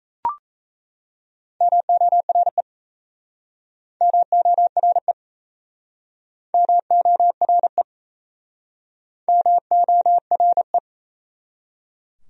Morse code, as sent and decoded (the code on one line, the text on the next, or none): -- --- .-. / -- --- .-. / -- --- .-. / -- --- .-. .
MORE MORE MORE MORE